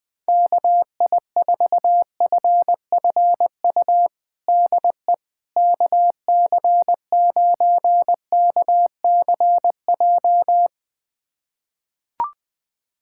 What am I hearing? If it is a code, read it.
Morse code, heard as a keyed tone, about 20 wpm, 700 Hz. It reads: KI4FFU DE KC9KCJ
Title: USA Callsign DE Callsign 20wpm